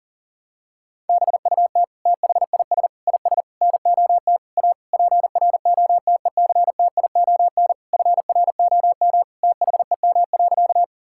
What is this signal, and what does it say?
Morse code, heard as a keyed tone, about 40 wpm, 700 Hz. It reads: BUT THIS IS NOT A PROTECTION FROM THEM.
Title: Sentences from Top 1600 Words 40wpm